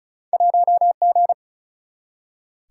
Morse code, heard as a keyed tone, about 36 wpm, 700 Hz. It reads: 1G
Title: Single-Letter Single-Number 35wpm